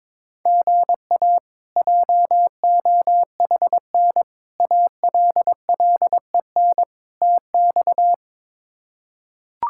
Morse code, heard as a keyed tone, about 22 wpm, 700 Hz. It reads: GA JOHN ALLEN TX
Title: K1USN SST QSO Practice - 22wpm